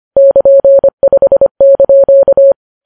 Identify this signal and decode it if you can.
Morse code, heard as a keyed tone, about 25 wpm, 550 Hz. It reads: <KN>5)